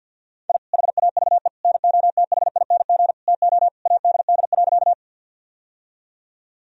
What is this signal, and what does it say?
Morse code, heard as a keyed tone, about 49 wpm, 700 Hz. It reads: I HAVE NOTHING TO ADD.
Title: Sentences from Top 1000 Words - Review Part 1 - 50wpm